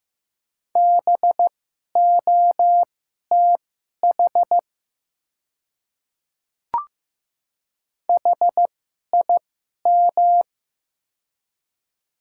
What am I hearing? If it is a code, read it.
Morse code, heard as a keyed tone, about 15 wpm, spaced as for 11 wpm, 700 Hz. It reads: BOTH HIM
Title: 2x Character Spacing - Top 300 Words - 15wpm